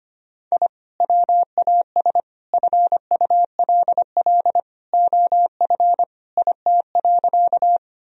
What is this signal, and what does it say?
Morse code, heard as a keyed tone, about 25 wpm, 700 Hz. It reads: I WAS FULL OF IT.
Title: Sentences from Top 1000 Words - Review Part 3 - 25wpm